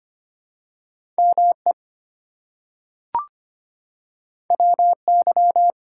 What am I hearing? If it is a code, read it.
Morse code, heard as a keyed tone, about 25 wpm, 700 Hz. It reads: ME WY
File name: practice / 25wpm / ARRL Sections v2 - 25wpm